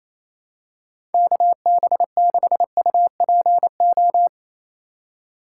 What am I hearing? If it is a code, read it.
Morse code, heard as a keyed tone, about 28 wpm, 700 Hz. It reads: KB6UPO
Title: USA Call Signs - 6 Characters - 4x Character Spacing - 28wpm